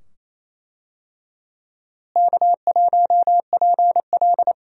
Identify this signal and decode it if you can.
Morse code, heard as a keyed tone, about 28 wpm, 700 Hz. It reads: K1PL
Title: USA Call Signs - 4 Characters - Repeated 2x - 28wpm